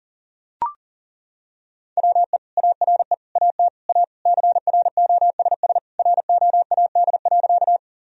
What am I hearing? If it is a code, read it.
Morse code, heard as a keyed tone, about 40 wpm, 700 Hz. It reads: WE ARE AT A CROSS ROAD.
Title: Sentences from Top 1000 Words - Review Part 2 - 40wpm